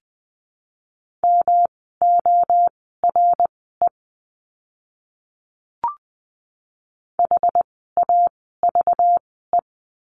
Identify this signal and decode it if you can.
Morse code, heard as a keyed tone, about 20 wpm, spaced as for 14 wpm, 700 Hz. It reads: MORE HAVE
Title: 244 Lesson 39f - Letter z - Words 20wpm